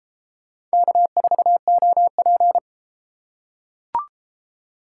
Morse code, heard as a keyed tone, about 30 wpm, 700 Hz. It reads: K4OP